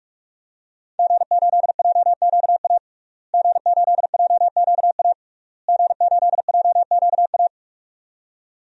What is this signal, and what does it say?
Morse code, heard as a keyed tone, about 42 wpm, 700 Hz. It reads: G8JQA G8JQA G8JQA
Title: DXCC-100 Call Signs - 5 Characters - Repeated 3x - 45wpm